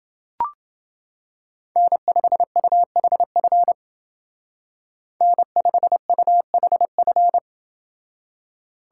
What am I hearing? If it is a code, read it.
Morse code, heard as a keyed tone, about 28 wpm, 700 Hz. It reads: N5UHF N5UHF